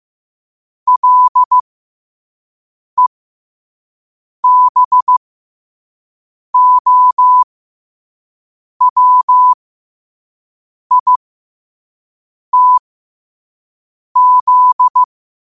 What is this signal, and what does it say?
Morse code, heard as a keyed tone, about 15 wpm, spaced as for 5 wpm, 1000 Hz. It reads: LEBOWITZ
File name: qotdfiles / 5wpm